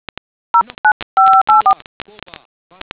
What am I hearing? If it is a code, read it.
DTMF, heard as telephone touch tones, tones of about 100 ms. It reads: 09584